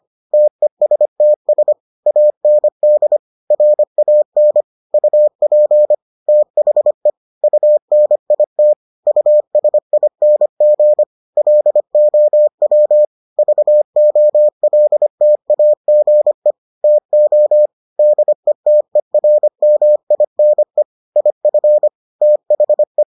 Morse code, heard as a keyed tone, about 25 wpm, 600 Hz. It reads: TESTS AND RAN UP THE UNIT USING LOW VOLTAGE TO DETERMINE IF THE